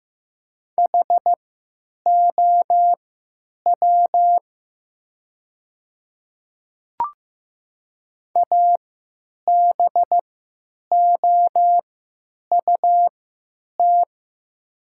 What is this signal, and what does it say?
Morse code, heard as a keyed tone, about 15 wpm, spaced as for 9 wpm, 700 Hz. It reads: HOW ABOUT
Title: 3x Character Spacing - Top 100 Words - 15wpm